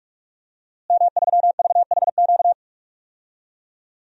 Morse code, heard as a keyed tone, about 46 wpm, 700 Hz. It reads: M3VHQ
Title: DXCC-100 Call Signs - 5 Characters - Repeated 2x - 45wpm